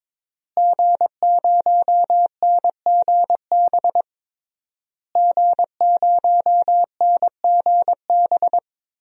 Morse code, heard as a keyed tone, about 22 wpm, 700 Hz. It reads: G0NGB G0NGB
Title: DXCC-100 Call Signs - 5 Characters - Repeated 3x - 22wpm